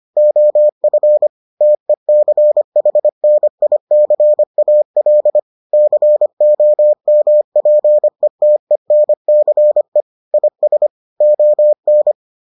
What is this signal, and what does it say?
Morse code, heard as a keyed tone, about 25 wpm, 600 Hz. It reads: OF TECHNICAL COMPETENCE IS ON